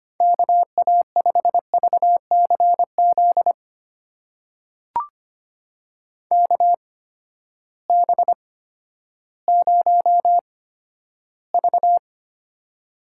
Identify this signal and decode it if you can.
Morse code, heard as a keyed tone, about 25 wpm, spaced as for 13 wpm, 700 Hz. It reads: KA5VCZ KB0V
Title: USA Call Signs - 8x Character Spacing - 25wpm